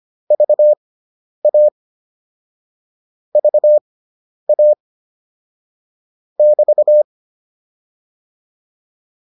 Morse code, heard as a keyed tone, about 25 wpm, spaced as for 10 wpm, 600 Hz. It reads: VA VA <BT>
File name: QSO208 10